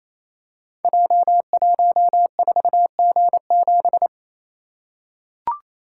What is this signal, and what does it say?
Morse code, heard as a keyed tone, about 28 wpm, 700 Hz. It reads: J14G7
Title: Code Groups - 5 Letter:Numbers - 28wpm